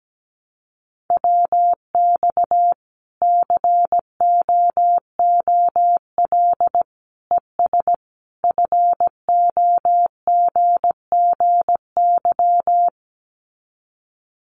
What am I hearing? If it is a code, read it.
Morse code, heard as a keyed tone, about 17 wpm, 700 Hz. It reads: WX COOL ES FOGGY
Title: Common QSO Phrases 17wpm